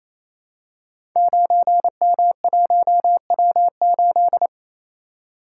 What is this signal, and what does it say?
Morse code, heard as a keyed tone, about 29 wpm, 700 Hz. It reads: 9M1W8
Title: Code Groups - 5 Letter:Numbers - 28wpm